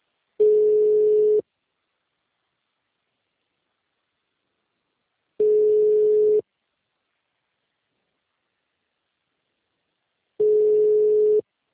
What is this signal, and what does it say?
Ringback tone (Europe, 425 Hz).